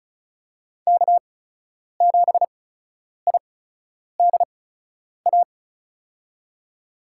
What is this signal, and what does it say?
Morse code, heard as a keyed tone, about 36 wpm, spaced as for 10 wpm, 700 Hz. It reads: K7IDA
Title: USA Call Signs - 5 Characters - 8x Character Spacing - 35wpm